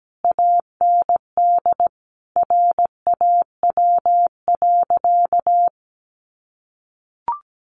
Morse code, heard as a keyed tone, about 17 wpm, 700 Hz. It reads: AND RAW.
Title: Sentences from Top 1500 Words 17wpm